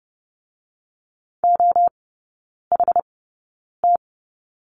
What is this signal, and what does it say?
Morse code, heard as a keyed tone, about 30 wpm, spaced as for 9 wpm, 700 Hz. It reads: OHT
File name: Lesson 16e - Letter c - Three Characters 30wpm